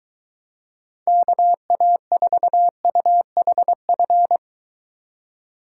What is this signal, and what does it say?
Morse code, heard as a keyed tone, about 22 wpm, 700 Hz. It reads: KA4UHF